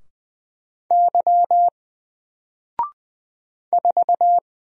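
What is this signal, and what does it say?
Morse code, heard as a keyed tone, about 20 wpm, 700 Hz. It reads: Y 4
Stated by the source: Single letter and number Morse drill at 20 wpm, from N4LCD.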